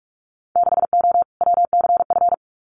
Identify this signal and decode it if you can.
Morse code, heard as a keyed tone, about 46 wpm, 700 Hz. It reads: 6O WCF
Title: Winter Field Day Exchanges Encore 45wpm